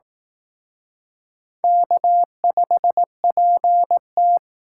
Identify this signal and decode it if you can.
Morse code, heard as a keyed tone, about 19 wpm, 700 Hz. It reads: K5PT
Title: Speed Racing - USA 1x2 Call Signs 15wpm